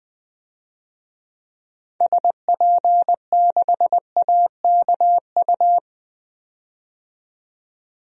Morse code, heard as a keyed tone, about 20 wpm, 700 Hz. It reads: SP6AKU
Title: DXCC-100 Call Signs - 6 Characters - 20wpm